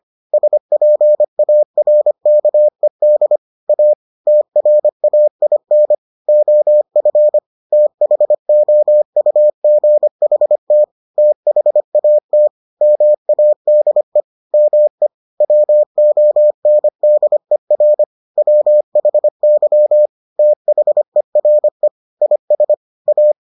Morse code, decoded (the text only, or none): SPARKED A TRAIN OF THOUGHT THAT MADE ME WONDER WHY THERE IS A